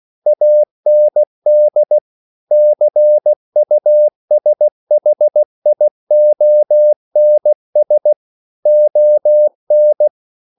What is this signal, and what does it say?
Morse code, heard as a keyed tone, about 16 wpm, 600 Hz. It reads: AND CUSHIONS ON